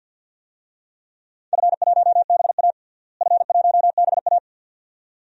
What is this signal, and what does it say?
Morse code, heard as a keyed tone, about 47 wpm, 700 Hz. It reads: F1BA F1BA
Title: DXCC-100 Call Signs - 4 Characters - Repeated 2x - 50wpm